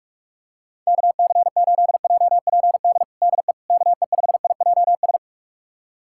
Morse code, heard as a keyed tone, about 42 wpm, 700 Hz. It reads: KC8JPD DE KE5IJS
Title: USA Callsign DE Callsign - 6 Characters - 45wpm